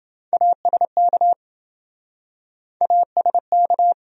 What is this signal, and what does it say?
Morse code, heard as a keyed tone, about 27 wpm, 700 Hz. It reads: ASK ASK